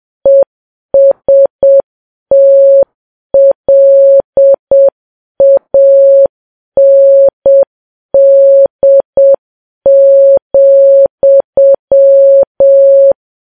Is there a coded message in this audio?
Yes: Morse code from 0.3 to 13.1 s, heard as a keyed tone, about 7 wpm, 550 Hz. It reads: ESTLAND,